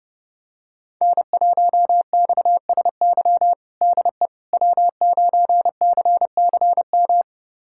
Morse code, heard as a keyed tone, about 30 wpm, 700 Hz. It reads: N1XSY DE W9CCM
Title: USA Callsign DE Callsign - 5 Characters - 30wpm